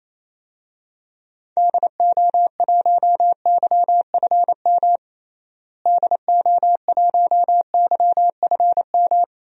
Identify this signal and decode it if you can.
Morse code, heard as a keyed tone, about 28 wpm, 700 Hz. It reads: DO1YFM DO1YFM
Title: DXCC-100 Call Signs - 6 Characters - Repeated 2x - 28wpm